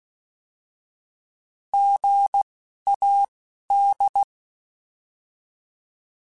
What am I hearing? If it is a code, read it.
Morse code, heard as a keyed tone, about 16 wpm, spaced as for 11 wpm, 800 Hz. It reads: GAD
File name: Alphabetical / Alphabetical mp3s / Gs @10